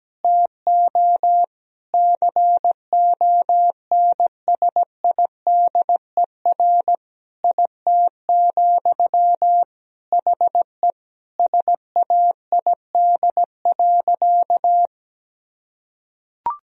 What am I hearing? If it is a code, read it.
Morse code, heard as a keyed tone, about 17 wpm, 700 Hz. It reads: TO CONSIDER IT, HE SAID.
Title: Sentences from Top 1000 Words - Review Part 4 - 17wpm